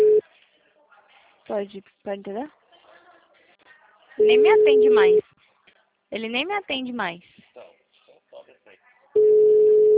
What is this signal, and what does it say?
Ringback tone (Europe, 425 Hz), with speech.